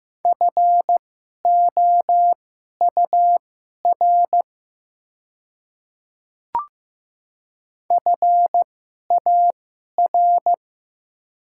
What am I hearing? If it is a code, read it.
Morse code, heard as a keyed tone, about 15 wpm, spaced as for 11 wpm, 690 Hz. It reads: FOUR FAR
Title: 2x Character Spacing - Top 300 Words - 15wpm